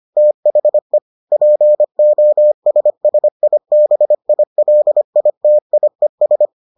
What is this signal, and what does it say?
Morse code, heard as a keyed tone, about 25 wpm, 600 Hz. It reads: THE POSSIBILITIES